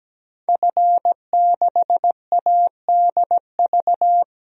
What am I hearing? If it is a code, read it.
Morse code, heard as a keyed tone, about 17 wpm, 700 Hz. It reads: F6ADV
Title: DXCC-100 Call Signs - 5 Characters - Repeated 3x - 17wpm